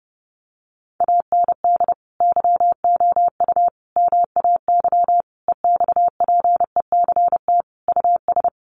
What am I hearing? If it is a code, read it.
Morse code, heard as a keyed tone, about 30 wpm, 700 Hz. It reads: AND YOU MAY EXPECT US
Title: Sentences from Top 1500 Words 30wpm